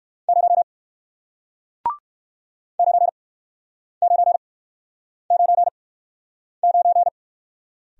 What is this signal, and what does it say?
Morse code, heard as a keyed tone, about 58 wpm, spaced as for 9 wpm, 700 Hz. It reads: 0 9999